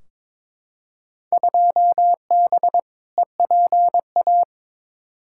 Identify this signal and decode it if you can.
Morse code, heard as a keyed tone, about 21 wpm, 700 Hz. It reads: 2B EPA
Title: Field Day Exchanges Encore - 22wpm